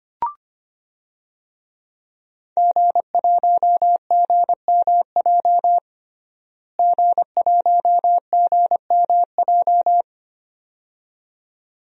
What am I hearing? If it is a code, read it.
Morse code, heard as a keyed tone, about 25 wpm, 700 Hz. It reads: G1GMJ G1GMJ
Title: DXCC-100 Call Signs - 5 Characters - Repeated 2x - 25wpm